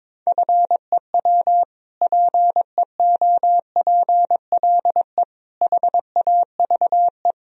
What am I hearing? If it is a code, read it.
Morse code, heard as a keyed tone, about 22 wpm, 700 Hz. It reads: FEW PEOPLE HAVE
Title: Sentences from Top 1000 Words - Review Part 1 - 22wpm